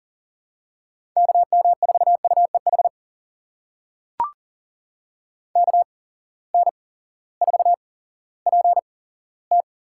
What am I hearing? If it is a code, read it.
Morse code, heard as a keyed tone, about 40 wpm, spaced as for 18 wpm, 700 Hz. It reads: KM4UEH KN4PT
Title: USA Call Signs - 6 Characters - 8x Character Spacing - 40wpm